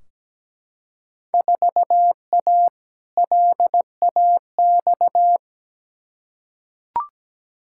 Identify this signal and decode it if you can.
Morse code, heard as a keyed tone, about 17 wpm, 700 Hz. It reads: 4A LAX
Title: Field Day Exchanges - 17wpm